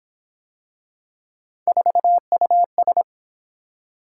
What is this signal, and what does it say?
Morse code, heard as a keyed tone, about 26 wpm, 700 Hz. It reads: 4US
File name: Speed-Racing - Three Letter-Number Combinations - 22wpm